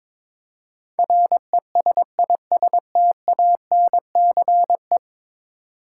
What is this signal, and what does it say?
Morse code, heard as a keyed tone, about 22 wpm, 700 Hz. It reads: RESISTANCE
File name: practice / 22wpm / 10 Letter Words - First 3 Letters Given - 22wpm